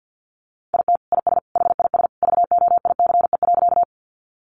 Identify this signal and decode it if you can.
Morse code, heard as a keyed tone, about 50 wpm, 700 Hz. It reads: IT IS HIS VOICE.
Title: Sentences from Top 500 Words 50wpm